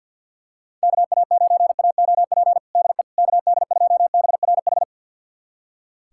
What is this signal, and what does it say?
Morse code, heard as a keyed tone, about 45 wpm, 690 Hz. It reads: KA9AOP DE KD2BRH
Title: USA Callsign DE Callsign - 6 Characters - 50wpm